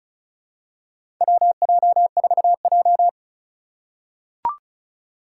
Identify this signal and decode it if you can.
Morse code, heard as a keyed tone, about 36 wpm, 700 Hz. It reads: WJ4J
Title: USA Call Signs - 4 Characters - 2x Character Spacing - 35wpm